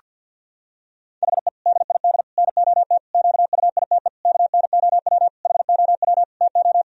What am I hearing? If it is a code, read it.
Morse code, heard as a keyed tone, about 48 wpm, 700 Hz. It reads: HE DID NOT QUITE KNOW HOW TO
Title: Sentences from Top 1000 Words - Review Part 1 - 50wpm